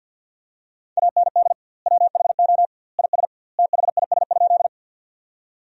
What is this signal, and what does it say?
Morse code, heard as a keyed tone, about 50 wpm, 700 Hz. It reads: AND WHO IS THIS?
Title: 2x Word-Spacing - Sentences from Top 100 Words - 50wpm